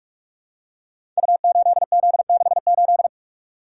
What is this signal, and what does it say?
Morse code, heard as a keyed tone, about 44 wpm, 700 Hz. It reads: U8Z68
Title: Code Groups - 5 Letter:Numbers - 45wpm